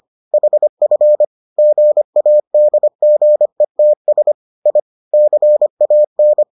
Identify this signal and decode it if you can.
Morse code, heard as a keyed tone, about 25 wpm, 600 Hz. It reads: HF GADGETS I CAN